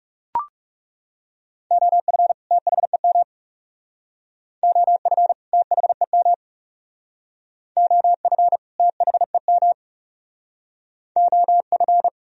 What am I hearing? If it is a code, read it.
Morse code, heard as a keyed tone, about 36 wpm, 700 Hz. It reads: OF THEM OF THEM OF THEM OF
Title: Speed-Racing - Sets of 2 Words 30wpm